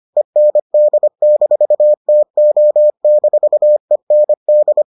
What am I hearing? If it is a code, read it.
Morse code, heard as a keyed tone, about 25 wpm, 600 Hz. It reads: END-TO-END